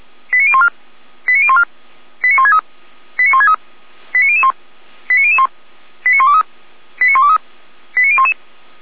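Five-tone selective call (ZVEI), heard as quick runs of five tones, tones of about 70 ms. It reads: 89014 89014 89163 89163 89001 89001 89123 89123 89011
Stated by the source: ZVEI five-tone call received by SDR, from the feuerwehr-orlamuende.org audio server.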